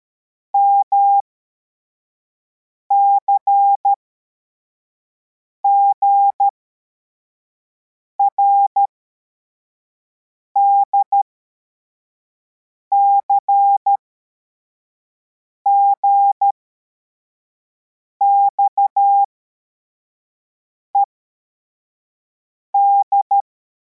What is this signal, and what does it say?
Morse code, heard as a keyed tone, about 13 wpm, spaced as for 4 wpm, 800 Hz. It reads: MCGRD CGXED